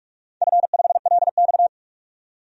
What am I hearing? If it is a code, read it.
Morse code, heard as a keyed tone, about 44 wpm, 700 Hz. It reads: F5LX
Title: DXCC-100 Call Signs - 4 Characters - 45wpm